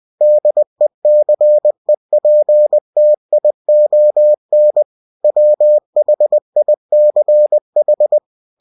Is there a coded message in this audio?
Yes: Morse code from 0.2 to 8.2 s, heard as a keyed tone, about 20 wpm, 600 Hz. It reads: DECEPTION WHICH